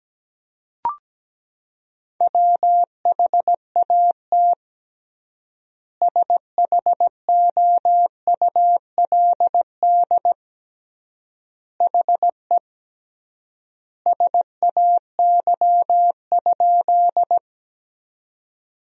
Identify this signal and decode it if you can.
Morse code, heard as a keyed tone, about 17 wpm, 700 Hz. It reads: WHAT SHOULD HE SAY?
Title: 3x Word-Spacing - Sentences from Top 300 Words - 17wpm